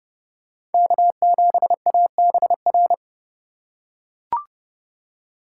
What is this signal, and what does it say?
Morse code, heard as a keyed tone, about 32 wpm, 700 Hz. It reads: K7ABR
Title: USA Call Signs - 5 Characters - 4x Character Spacing - 30wpm